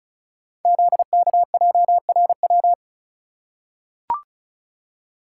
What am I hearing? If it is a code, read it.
Morse code, heard as a keyed tone, about 36 wpm, 690 Hz. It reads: ZKJRW
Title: Code Groups - 5 Letter:Numbers - 35wpm